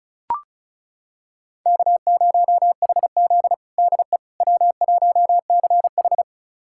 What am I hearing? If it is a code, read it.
Morse code, heard as a keyed tone, about 36 wpm, 700 Hz. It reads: K0HZ DE W1CH
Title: USA Callsign DE Callsign - 4 Characters - 35wpm